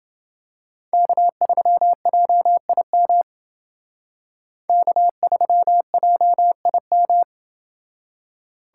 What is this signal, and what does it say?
Morse code, heard as a keyed tone, about 28 wpm, 700 Hz. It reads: K3JIM K3JIM